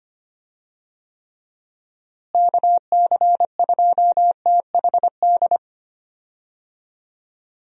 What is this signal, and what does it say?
Morse code, heard as a keyed tone, about 25 wpm, 700 Hz. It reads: KC2THD
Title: USA Call Signs - 6 Characters - Repeated 3x - 25wpm